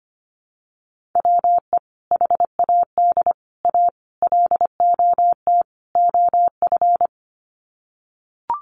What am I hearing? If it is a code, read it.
Morse code, heard as a keyed tone, about 25 wpm, 700 Hz. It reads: WE HAD A LOT OF
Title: Sets of 5 Words Encore 25wpm